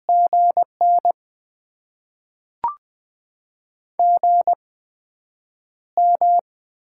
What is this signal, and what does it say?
Morse code, heard as a keyed tone, about 20 wpm, spaced as for 9 wpm, 700 Hz. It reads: GN GM
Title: Top 20 QSO Elements - 8x Character Spacing - 20wpm